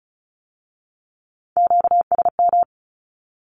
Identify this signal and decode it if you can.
Morse code, heard as a keyed tone, about 36 wpm, 700 Hz. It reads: QSM
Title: Q-Codes with Spoken Meaning - 35wpm